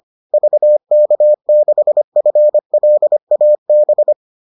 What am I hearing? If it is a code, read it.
Morse code, heard as a keyed tone, about 25 wpm, 600 Hz. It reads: VK6FLAB